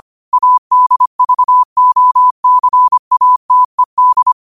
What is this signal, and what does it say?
Morse code, heard as a keyed tone, about 25 wpm, 1000 Hz. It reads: ADVOCATED